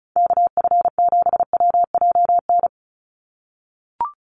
Morse code, heard as a keyed tone, about 34 wpm, 700 Hz. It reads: KF7WJN
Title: USA Call Signs 35wpm